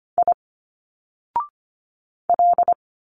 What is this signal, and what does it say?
Morse code, heard as a keyed tone, about 25 wpm, 700 Hz. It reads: I L
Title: Single Letters 25wpm